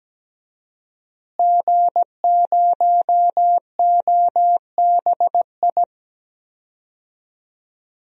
Morse code, heard as a keyed tone, about 17 wpm, 700 Hz. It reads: G0OBI
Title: DXCC-100 Call Signs - 5 Characters - Repeated 2x - 17wpm